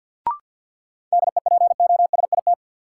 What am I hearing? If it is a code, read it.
Morse code, heard as a keyed tone, about 52 wpm, 700 Hz. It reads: DEPOSIT
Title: Sentences from Top 1300 Words - New Word Drill Alt - 50wpm